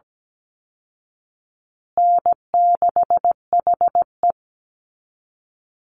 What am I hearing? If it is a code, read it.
Morse code, heard as a keyed tone, about 17 wpm, 700 Hz. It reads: N6HE
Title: Easy Call Signs Repeated 3x - 17wpm